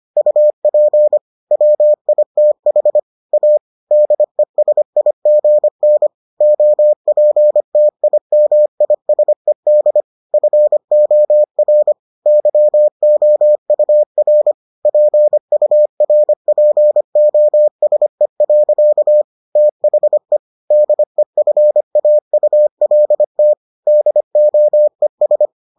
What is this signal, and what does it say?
Morse code, heard as a keyed tone, about 25 wpm, 600 Hz. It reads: UP WITH A DESIGN OPTIMISED FOR YOUR PURPOSE. THE DEFAULT DOES